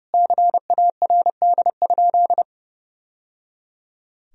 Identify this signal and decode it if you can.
Morse code, heard as a keyed tone, about 30 wpm, 700 Hz. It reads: CARD?
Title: Sentences from Top 1000 Words - Review Part 4 - 30wpm